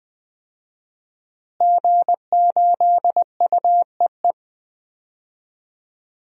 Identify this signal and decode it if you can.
Morse code, heard as a keyed tone, about 20 wpm, 700 Hz. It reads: G8UEE
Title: DXCC-100 Call Signs - 5 Characters - Repeated 3x - 20wpm